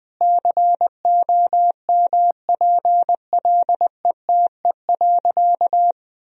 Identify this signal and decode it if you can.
Morse code, heard as a keyed tone, about 21 wpm, 700 Hz. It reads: COMPLETE.
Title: Sentences from Top 1000 Words - Review Part 2 - 20wpm